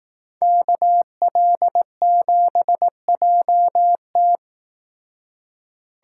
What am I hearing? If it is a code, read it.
Morse code, heard as a keyed tone, about 18 wpm, 700 Hz. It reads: KL7JT
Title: Speed Racing - USA 2x2 Call Signs 15wpm